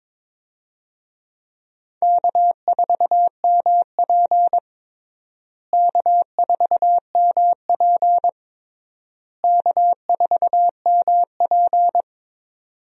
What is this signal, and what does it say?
Morse code, heard as a keyed tone, about 22 wpm, 700 Hz. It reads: K4MP K4MP K4MP
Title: USA Call Signs - 4 Characters - Repeated 3x - 22wpm